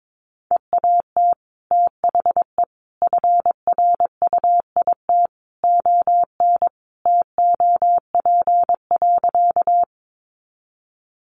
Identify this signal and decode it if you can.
Morse code, heard as a keyed tone, about 22 wpm, 700 Hz. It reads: EAT THE FRUIT ON TOP.
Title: Sentences from Top 1300 Words 22wpm